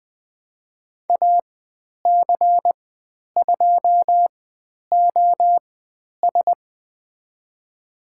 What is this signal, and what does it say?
Morse code, heard as a keyed tone, about 20 wpm, spaced as for 10 wpm, 700 Hz. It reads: AC2OS
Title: 095 Lesson 16g - Letter c - Call Signs 20wpm